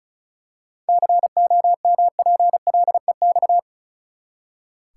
Morse code, heard as a keyed tone, about 34 wpm, 700 Hz. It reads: COMPLEX
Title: Seven Letter Words - 35wpm